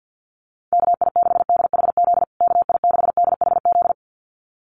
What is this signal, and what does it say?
Morse code, heard as a keyed tone, about 49 wpm, 700 Hz. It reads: KI6DHZ KI6DHZ
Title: USA Call Signs Repeated 2x 50wpm